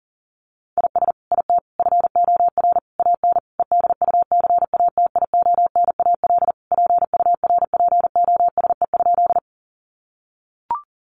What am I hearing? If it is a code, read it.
Morse code, heard as a keyed tone, about 40 wpm, 700 Hz. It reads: IS IT FOR AN EDUCATIONAL PURPOSE?
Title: Sentences from Top 1200 Words 40wpm